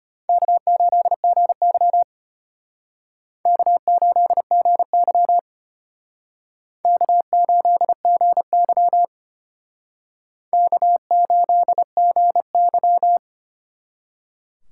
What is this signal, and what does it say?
Morse code, heard as a keyed tone, about 34 wpm, 700 Hz. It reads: K8GY K8GY K8GY K8GY